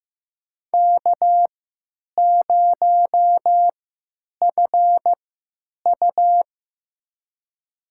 Morse code, heard as a keyed tone, about 15 wpm, spaced as for 9 wpm, 700 Hz. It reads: K0FU
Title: USA Call Signs - 4 Characters - 3x Character Spacing - 15wpm